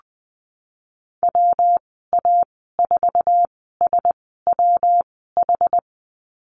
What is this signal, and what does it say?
Morse code, heard as a keyed tone, about 20 wpm, spaced as for 15 wpm, 700 Hz. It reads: WA4SWH
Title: 147 Lesson 24g - Letter y - Call Signs 20wpm